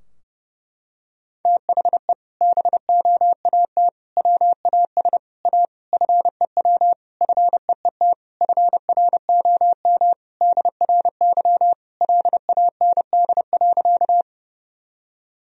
Morse code, heard as a keyed tone, about 30 wpm, 700 Hz. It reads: THE BOAT WAS A FEW FEET FROM DRY LAND.
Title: Sentences from Top 1000 Words - Review Part 3 - 30wpm